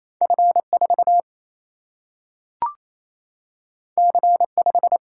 Morse code, heard as a keyed tone, about 28 wpm, 700 Hz. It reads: F4 C5
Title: Single-Letter Single-Number 28wpm